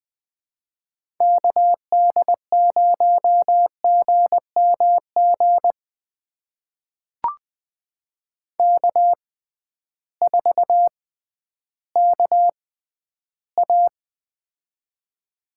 Morse code, heard as a keyed tone, about 20 wpm, spaced as for 12 wpm, 700 Hz. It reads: KD0GMG K4KA
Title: USA Call Signs - 6x Character Spacing - 20wpm